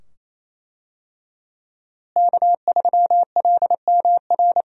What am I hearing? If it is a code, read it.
Morse code, heard as a keyed tone, about 28 wpm, 700 Hz. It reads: K3LMR
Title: USA Call Signs - 5 Characters - Repeated 2x - 28wpm